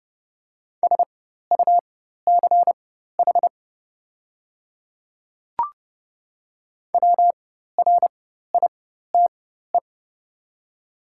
Morse code, heard as a keyed tone, about 29 wpm, spaced as for 14 wpm, 690 Hz. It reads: SUCH WRITE
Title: 4x Character Spacing - Top 200 Words - 30wpm